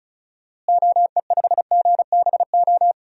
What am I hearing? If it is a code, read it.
Morse code, heard as a keyed tone, about 36 wpm, 700 Hz. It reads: OE5GBO
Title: DXCC-100 Call Signs - 6 Characters - Repeated 2x - 35wpm